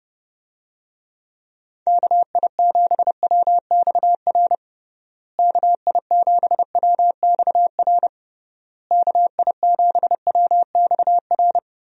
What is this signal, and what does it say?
Morse code, heard as a keyed tone, about 30 wpm, 700 Hz. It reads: KI7WXR KI7WXR KI7WXR
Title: USA Call Signs - 6 Characters - Repeated 3x - 30wpm